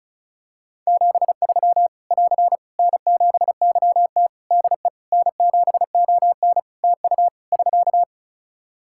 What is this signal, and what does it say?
Morse code, heard as a keyed tone, about 34 wpm, 700 Hz. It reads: 73 <AR> N7YT DE N7ON TU <SK>
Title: Practice QSO - Exchange 4 - Ending - 35wpm